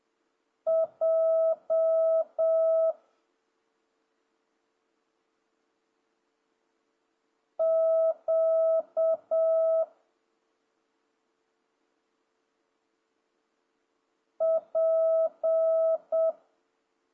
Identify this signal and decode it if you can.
Morse code, heard as a keyed tone, about 7 wpm, 650 Hz. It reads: J Q P
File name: jh8cbh / 2026cw